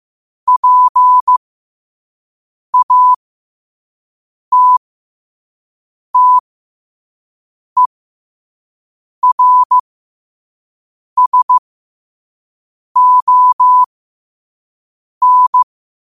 Morse code, heard as a keyed tone, about 15 wpm, spaced as for 5 wpm, 1000 Hz. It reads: PATTERSON
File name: qotdfiles / 5wpm